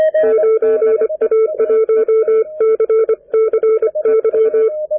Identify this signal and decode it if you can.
Morse code, heard as a keyed tone, about 25 wpm, 440 Hz. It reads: 9A1CCY
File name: n6tr / N6TR at TF4M 10-Jan-2010